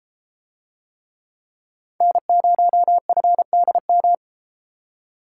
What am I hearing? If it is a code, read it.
Morse code, heard as a keyed tone, about 31 wpm, 700 Hz. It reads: N0FDM